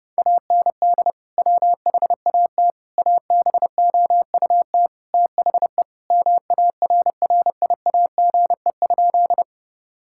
Morse code, heard as a keyed tone, about 32 wpm, 700 Hz. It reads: AND WHAT ABOUT THE MARRIAGE?
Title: Sentences from Top 1000 Words - Review Part 5 - 30wpm